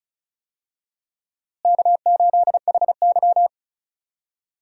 Morse code, heard as a keyed tone, about 34 wpm, 700 Hz. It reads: K8HY
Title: USA Call Signs - 4 Characters - Repeated 2x - 35wpm